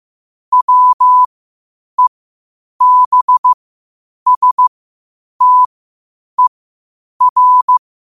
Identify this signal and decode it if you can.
Morse code, heard as a keyed tone, about 15 wpm, spaced as for 8 wpm, 1000 Hz. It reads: WEBSTER
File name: qotdfiles / 8wpm